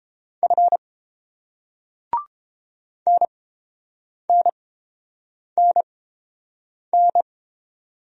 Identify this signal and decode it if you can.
Morse code, heard as a keyed tone, about 29 wpm, spaced as for 7 wpm, 690 Hz. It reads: F NNNN